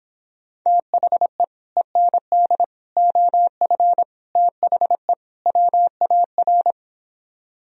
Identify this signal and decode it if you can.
Morse code, heard as a keyed tone, about 26 wpm, 700 Hz. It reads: THE END OF THE WAR